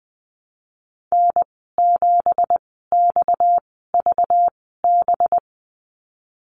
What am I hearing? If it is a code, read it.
Morse code, heard as a keyed tone, about 19 wpm, spaced as for 14 wpm, 700 Hz. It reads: N7XVB
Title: 238 Lesson 38g - Letter q - Call Signs 20wpm